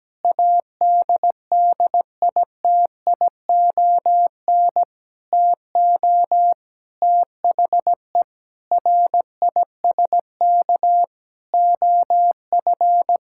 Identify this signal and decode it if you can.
Morse code, heard as a keyed tone, about 17 wpm, 700 Hz. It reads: ADDITION TO THE RISK OF
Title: Sentences from Top 1000 Words - Review Part 6 - 17wpm